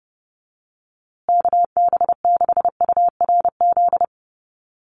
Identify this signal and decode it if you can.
Morse code, heard as a keyed tone, about 30 wpm, 700 Hz. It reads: KB6URZ
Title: USA Call Signs Repeated 3x 30wpm